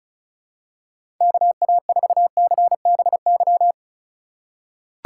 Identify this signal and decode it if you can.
Morse code, heard as a keyed tone, about 36 wpm, 700 Hz. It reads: KA4CBY
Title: USA Call Signs - 6 Characters - 3x Character Spacing - 35wpm